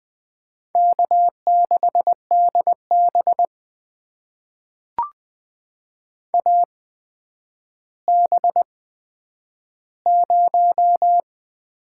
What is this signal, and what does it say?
Morse code, heard as a keyed tone, about 21 wpm, spaced as for 10 wpm, 700 Hz. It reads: K6DB AB0
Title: USA Call Signs - 4 Characters - 8x Character Spacing - 20wpm